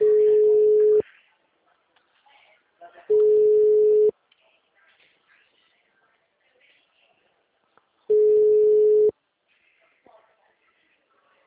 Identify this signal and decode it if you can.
Ringback tone (Europe, 425 Hz), with speech.
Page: Araras (FGTS)